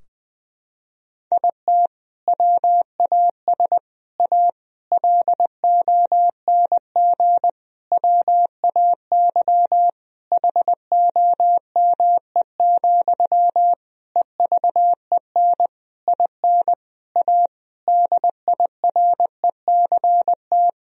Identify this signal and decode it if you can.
Morse code, heard as a keyed tone, about 20 wpm, 700 Hz. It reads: IT WAS A LONG WAY HOME, EVEN IN A DIRECT
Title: Sentences from Top 1000 Words - Review Part 2 - 20wpm